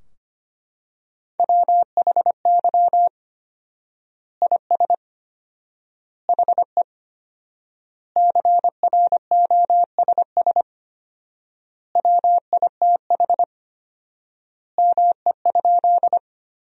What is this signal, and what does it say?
Morse code, heard as a keyed tone, about 25 wpm, 700 Hz. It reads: WHY IS HE CROSS WITH ME?
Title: 4x Word-Spacing - Sentences from Top 300 Words - 25wpm